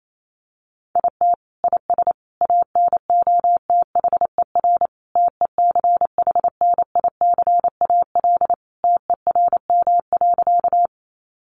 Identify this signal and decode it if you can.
Morse code, heard as a keyed tone, about 28 wpm, 700 Hz. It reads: IT IS ANOTHER TECHNICAL TERM.
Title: Sentences from Top 1100 Words 28wpm